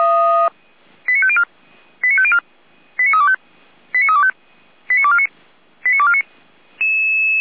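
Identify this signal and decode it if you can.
Five-tone selective call (ZVEI), heard as quick runs of five tones, tones of about 70 ms. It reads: 89504 89504 89325 89325 89249 89249
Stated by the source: ZVEI five-tone selective call 89250, received with an SDR, archived by Feuerwehr Orlamünde.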